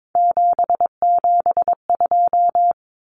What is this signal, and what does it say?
Morse code, heard as a keyed tone, about 22 wpm, 700 Hz. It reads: 772
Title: Numbers - 3-Digit 22wpm